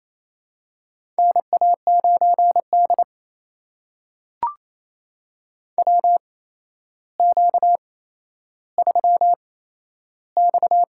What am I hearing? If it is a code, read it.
Morse code, heard as a keyed tone, about 28 wpm, spaced as for 12 wpm, 700 Hz. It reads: NA9D WQ3X
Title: USA Call Signs - 8x Character Spacing - 28wpm